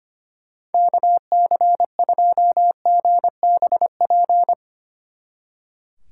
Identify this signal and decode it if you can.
Morse code, heard as a keyed tone, about 32 wpm, 700 Hz. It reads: KC2GBP